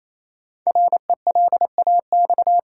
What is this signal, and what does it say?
Morse code, heard as a keyed tone, about 28 wpm, 700 Hz. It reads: RELAX
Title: Sentences from Top 1400 Words - New Word Drill - 28wpm